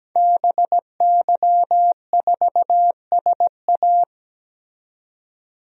Morse code, heard as a keyed tone, about 17 wpm, 700 Hz. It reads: BY4SA
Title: DXCC-100 Call Signs - 5 Characters - 17wpm